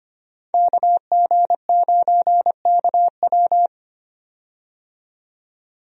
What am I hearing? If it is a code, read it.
Morse code, heard as a keyed tone, about 25 wpm, 700 Hz. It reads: KG9KW
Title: USA Call Signs - 5 Characters - 25wpm